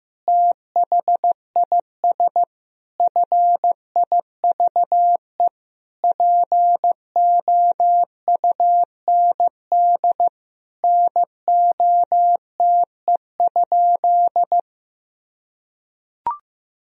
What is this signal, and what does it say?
Morse code, heard as a keyed tone, about 15 wpm, 700 Hz. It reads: THIS FIVE POUND NOTE?
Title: Sentences from Top 1000 Words - Review Part 2 - 15wpm